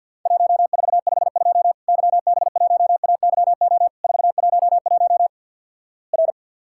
Morse code, heard as a keyed tone, about 50 wpm, 700 Hz. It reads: 1452 YB1AYO 411 R
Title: Worked All Europe DX Contest - QTC Exchange - 50wpm